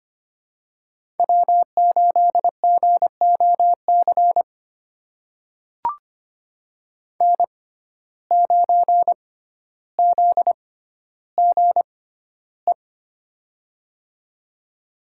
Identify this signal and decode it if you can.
Morse code, heard as a keyed tone, about 25 wpm, spaced as for 13 wpm, 700 Hz. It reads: W8GOC N9ZGE
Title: USA Call Signs - 6x Character Spacing - 25wpm